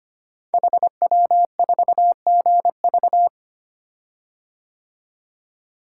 Morse code, heard as a keyed tone, about 25 wpm, 700 Hz. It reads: HW4GV
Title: Code Groups - 5 Letter:Numbers - 25wpm